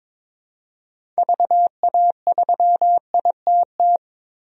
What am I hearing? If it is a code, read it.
Morse code, heard as a keyed tone, about 23 wpm, 700 Hz. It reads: VA3ITT
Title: DXCC-100 Call Signs - 6 Characters - 22wpm